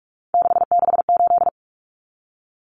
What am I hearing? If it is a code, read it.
Morse code, heard as a keyed tone, about 44 wpm, 700 Hz. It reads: 668
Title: Numbers - 3-Digit 45wpm